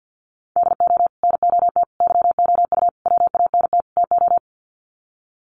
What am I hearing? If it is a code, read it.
Morse code, heard as a keyed tone, about 49 wpm, 700 Hz. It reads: DO NOT YOU WANT TO
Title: Sets of 5 Words Encore2 50wpm